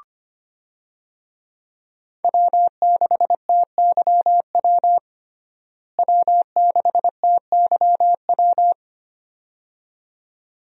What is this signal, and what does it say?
Morse code, heard as a keyed tone, about 25 wpm, 700 Hz. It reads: W6TYW W6TYW
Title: USA Call Signs - 5 Characters - Repeated 2x - 25wpm